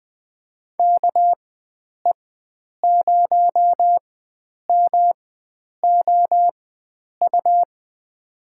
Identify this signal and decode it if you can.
Morse code, heard as a keyed tone, about 20 wpm, spaced as for 9 wpm, 700 Hz. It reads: KE0MOU
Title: USA Call Signs - 4x Character Spacing - 20wpm